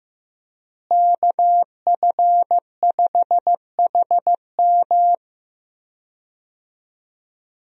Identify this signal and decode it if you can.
Morse code, heard as a keyed tone, about 15 wpm, 700 Hz. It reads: KF5HM
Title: USA Call Signs - 5 Characters - Repeated 3x - 15wpm